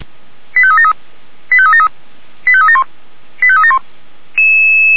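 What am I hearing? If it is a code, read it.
Five-tone selective call (ZVEI), heard as quick runs of five tones, tones of about 80 ms. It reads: 85373 85373 85371 85371
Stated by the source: ZVEI five-tone call received by SDR, from the Feuerwehr Orlamünde audio archive.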